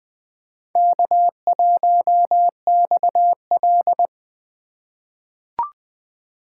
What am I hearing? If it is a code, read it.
Morse code, heard as a keyed tone, about 21 wpm, 700 Hz. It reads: K1XL
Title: USA Call Signs - 4 Characters - 4x Character Spacing - 20wpm